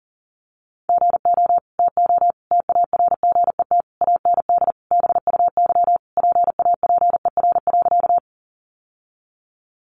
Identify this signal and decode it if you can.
Morse code, heard as a keyed tone, about 40 wpm, 700 Hz. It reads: GO TO TARGET AND BUY PAPER.
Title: Sentences from Top 1200 Words 40wpm